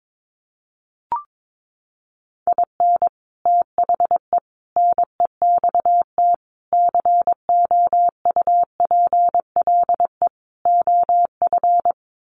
Morse code, heard as a keyed tone, about 22 wpm, 700 Hz. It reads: IN THE NEXT COUPLE OF
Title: Sets of 5 Words Encore2 22wpm